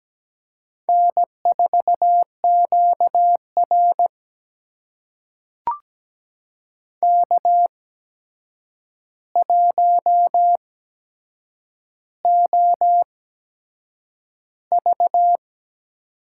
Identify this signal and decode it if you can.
Morse code, heard as a keyed tone, about 17 wpm, spaced as for 7 wpm, 700 Hz. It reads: N4QR K1OV
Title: USA Call Signs - 4 Characters - 8x Character Spacing - 17wpm